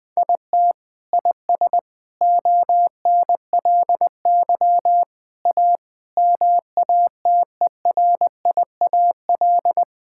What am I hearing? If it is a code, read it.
Morse code, heard as a keyed tone, about 21 wpm, 700 Hz. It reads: IT IS ONLY A MATERIAL
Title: Sentences from Top 1000 Words - Review Part 5 - 20wpm